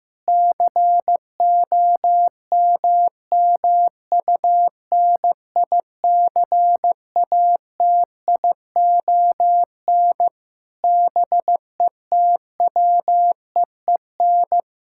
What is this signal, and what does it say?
Morse code, heard as a keyed tone, about 15 wpm, 700 Hz. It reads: COMMUNICATION BETWEEN
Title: Sentences from Top 1000 Words - Review Part 6 - 15wpm